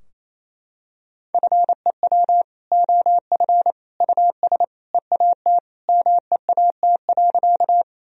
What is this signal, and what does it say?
Morse code, heard as a keyed tone, about 28 wpm, 700 Hz. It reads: FEW OF US EAT MEAT.
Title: Sentences from Top 1000 Words - Review Part 3 - 28wpm